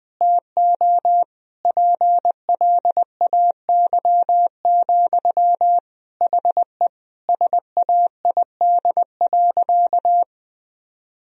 Morse code, heard as a keyed tone, about 21 wpm, 700 Hz. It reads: TO PLAY, HE SAID.
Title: Sentences from Top 1000 Words - Review Part 4 - 20wpm